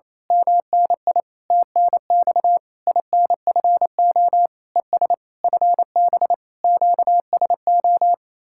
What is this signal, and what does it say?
Morse code, heard as a keyed tone, about 26 wpm, 700 Hz. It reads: MNI TNX INFO ES FB QSO
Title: Speed-Racing Common QSO Phrases Encore - 28wpm